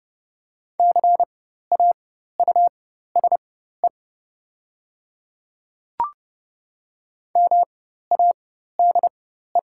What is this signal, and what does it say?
Morse code, heard as a keyed tone, about 29 wpm, spaced as for 14 wpm, 700 Hz. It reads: CAUSE MADE
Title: 4x Character Spacing - Top 200 Words - 30wpm